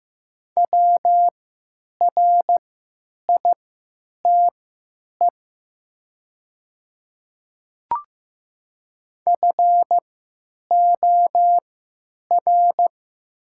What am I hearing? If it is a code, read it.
Morse code, heard as a keyed tone, about 15 wpm, spaced as for 9 wpm, 700 Hz. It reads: WRITE FOR
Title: 3x Character Spacing - Top 300 Words - 15wpm